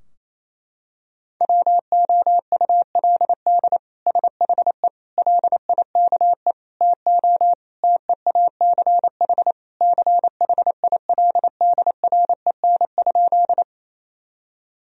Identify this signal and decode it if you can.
Morse code, heard as a keyed tone, about 28 wpm, 700 Hz. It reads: WOULD SHE LIKE TO TEACH CHILDREN?
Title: Sentences from Top 1000 Words - Review Part 2 - 28wpm